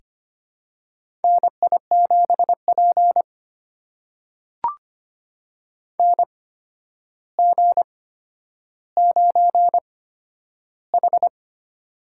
Morse code, heard as a keyed tone, about 25 wpm, spaced as for 11 wpm, 700 Hz. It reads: NI7P NG9H
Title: USA Call Signs - 4 Characters - 8x Character Spacing - 25wpm